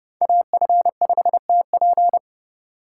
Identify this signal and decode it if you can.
Morse code, heard as a keyed tone, about 31 wpm, 700 Hz. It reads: AF5TP
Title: USA Call Signs - 5 Characters - 2x Character Spacing - 30wpm